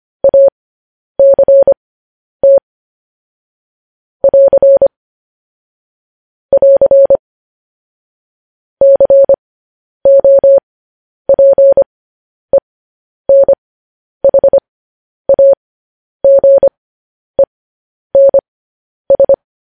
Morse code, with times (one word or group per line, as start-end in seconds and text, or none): 0.2-2.6 ACT
4.2-4.9 <AR>
6.5-7.2 <AR>
8.8-19.3 COPENHAGENS